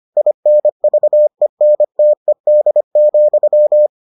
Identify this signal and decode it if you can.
Morse code, heard as a keyed tone, about 25 wpm, 600 Hz. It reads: INVENTED,